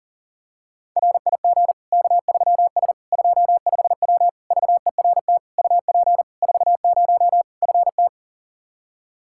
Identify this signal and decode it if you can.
Morse code, heard as a keyed tone, about 40 wpm, 700 Hz. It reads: RIG K3S 25W VERT UP 40 FT
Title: K1USN SST QSO Practice 40wpm